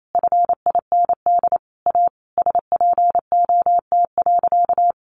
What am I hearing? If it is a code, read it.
Morse code, heard as a keyed tone, about 28 wpm, 700 Hz. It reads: FIND A SPOT.
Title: Sentences from Top 1100 Words 28wpm